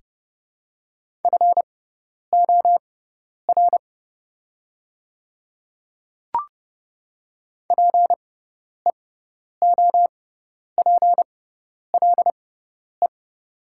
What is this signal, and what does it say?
Morse code, heard as a keyed tone, about 30 wpm, spaced as for 10 wpm, 700 Hz. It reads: FOR PEOPLE
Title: 6x Character Spacing - Top 200 Words - 30wpm